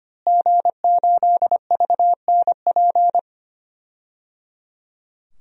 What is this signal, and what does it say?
Morse code, heard as a keyed tone, about 25 wpm, 700 Hz. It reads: G8VNP
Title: DXCC-100 Call Signs - 5 Characters - Repeated 3x - 25wpm